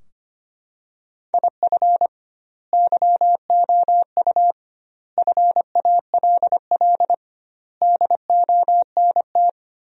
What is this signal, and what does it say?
Morse code, heard as a keyed tone, about 25 wpm, 700 Hz. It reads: IF YOU FALL DONT
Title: 2x Word-Spacing - Sentences from Top 500 Words - 25wpm